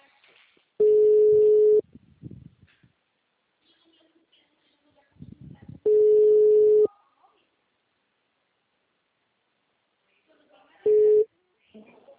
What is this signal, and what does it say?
Ringback tone (Europe, 425 Hz), with speech.
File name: out- -2011- - - .45967